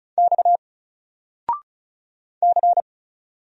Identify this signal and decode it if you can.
Morse code, heard as a keyed tone, about 36 wpm, 700 Hz. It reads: X C
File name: Single-Letter-Number-35wpm